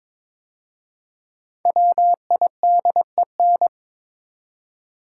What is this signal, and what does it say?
Morse code, heard as a keyed tone, about 22 wpm, 700 Hz. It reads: WIDEN
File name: 5 Letter Words - First 3 Letters Given - 22wpm